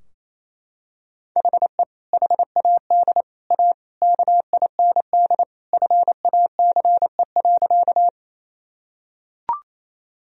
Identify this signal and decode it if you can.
Morse code, heard as a keyed tone, about 28 wpm, 700 Hz. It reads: HE HAD A KIND FACE.
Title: Sentences from Top 1000 Words - Review Part 2 - 28wpm